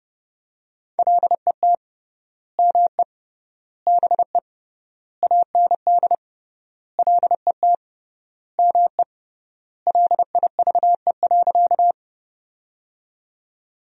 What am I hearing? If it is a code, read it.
Morse code, heard as a keyed tone, about 30 wpm, 700 Hz. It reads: LET ME BE AND LET ME LIVE.
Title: 3x Word-Spacing - Sentences from Top 300 Words - 30wpm